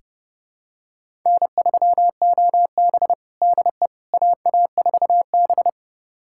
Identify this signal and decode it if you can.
Morse code, heard as a keyed tone, about 30 wpm, 700 Hz. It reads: N3OB DE AA4B
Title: USA Callsign DE Callsign - 4 Characters - 30wpm